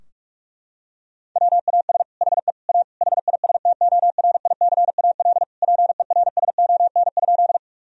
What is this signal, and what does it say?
Morse code, heard as a keyed tone, about 46 wpm, 700 Hz. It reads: WAS HE A HISTORICAL PERSON?
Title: Sentences from Top 1000 Words - Review Part 4 - 45wpm